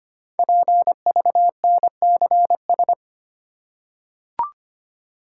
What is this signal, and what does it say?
Morse code, heard as a keyed tone, about 25 wpm, 700 Hz. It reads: PVNCS
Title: Code Groups - 5 Letter:Numbers - 25wpm